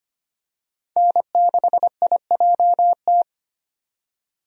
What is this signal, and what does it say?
Morse code, heard as a keyed tone, about 25 wpm, 690 Hz. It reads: N6IJT
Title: USA Call Signs - 5 Characters - 25wpm